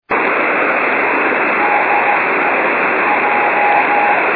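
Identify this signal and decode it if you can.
Morse code, heard as a keyed tone, about 15 wpm, 820 Hz, with noise. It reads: DE J
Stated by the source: 5.6 GHz Morse beacon JH1UGF, Roppongi.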